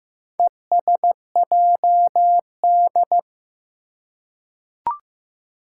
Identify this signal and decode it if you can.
Morse code, heard as a keyed tone, about 15 wpm, 700 Hz. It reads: ESJD
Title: Four Challenging Letter-Number Combinations 15wpm